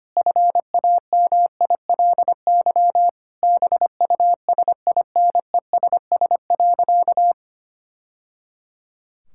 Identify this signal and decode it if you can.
Morse code, heard as a keyed tone, about 25 wpm, 700 Hz. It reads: FAMILY BUSINESS.
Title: Sentences from Top 1000 Words - Review Part 6 - 25wpm